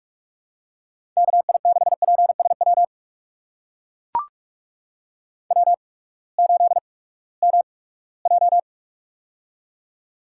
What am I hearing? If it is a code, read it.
Morse code, heard as a keyed tone, about 44 wpm, spaced as for 22 wpm, 700 Hz. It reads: KI6PSW W8MJ
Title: USA Call Signs - 8x Character Spacing - 45wpm